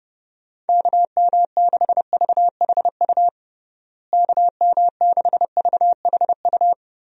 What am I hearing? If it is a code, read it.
Morse code, heard as a keyed tone, about 32 wpm, 700 Hz. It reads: KM6VHU KM6VHU
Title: USA Call Signs - 6 Characters - Repeated 3x - 30wpm